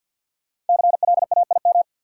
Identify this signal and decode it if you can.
Morse code, heard as a keyed tone, about 50 wpm, 700 Hz. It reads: CLAIM
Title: Sentences from Top 1500 Words - New Word Drill Alt - 50wpm